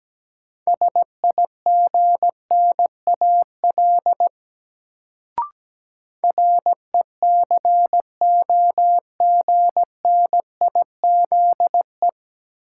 Morse code, heard as a keyed tone, about 17 wpm, 700 Hz. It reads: SIGNAL RECOGNIZE
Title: Sentences from Top 1400 Words - New Word Drill - 17wpm